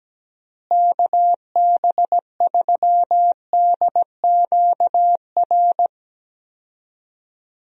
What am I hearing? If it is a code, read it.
Morse code, heard as a keyed tone, about 17 wpm, 700 Hz. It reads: KB3DQR
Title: USA Call Signs - 6 Characters - Repeated 3x - 17wpm